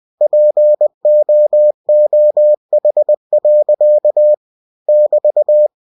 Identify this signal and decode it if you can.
Morse code, heard as a keyed tone, about 20 wpm, 600 Hz. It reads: POOH. <BT>
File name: cw-books / winnie-the-pooh / winnie-20wpm-page-0066